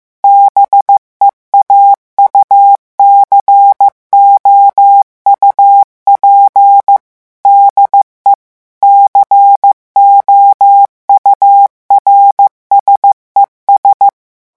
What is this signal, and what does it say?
Morse code, heard as a keyed tone, about 15 wpm, 800 Hz. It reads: BEAUCOUP DE COURSES
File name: morse mp3 / revision9 citations humoristiques vitesse 14 mots minute